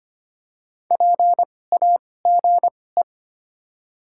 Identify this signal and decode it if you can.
Morse code, heard as a keyed tone, about 25 wpm, spaced as for 18 wpm, 700 Hz. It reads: PAGE
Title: 2x Character Spacing - Top 300 Words - 25wpm